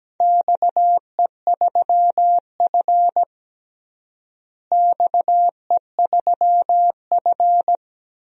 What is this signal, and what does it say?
Morse code, heard as a keyed tone, about 17 wpm, 700 Hz. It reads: XE3F XE3F
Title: DXCC-100 Call Signs - 4 Characters - Repeated 2x - 17wpm